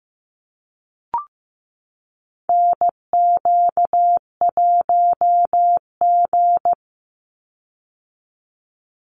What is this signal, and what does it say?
Morse code, heard as a keyed tone, about 15 wpm, 700 Hz. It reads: NQ1G
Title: USA Call Signs 15wpm